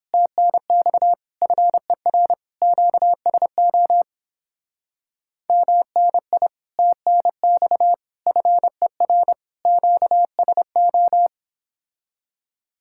Morse code, heard as a keyed tone, about 31 wpm, 700 Hz. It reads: TNX FER QSO MNI TNX FER QSO